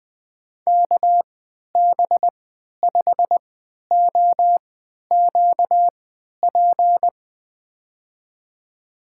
Morse code, heard as a keyed tone, about 20 wpm, spaced as for 11 wpm, 700 Hz. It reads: KB5OQP
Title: USA Call Signs - 6 Characters - 3x Character Spacing - 20wpm